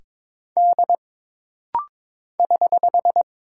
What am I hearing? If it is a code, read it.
Morse code, heard as a keyed tone, about 22 wpm, 700 Hz. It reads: D <HH>
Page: This rapid-fire timing is geared to push your ICR (Instant Character Recognition) skills.